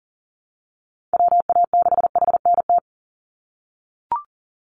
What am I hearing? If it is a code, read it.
Morse code, heard as a keyed tone, about 40 wpm, 700 Hz. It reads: WA6HNT
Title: USA Call Signs 40wpm